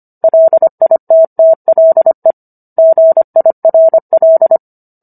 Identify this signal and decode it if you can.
Morse code, heard as a keyed tone, about 25 wpm, 650 Hz. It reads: LITTLE GIRL